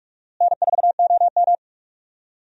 Morse code, heard as a keyed tone, about 42 wpm, 700 Hz. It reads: N4OM
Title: USA Call Signs - 4 Characters - Repeated 2x - 45wpm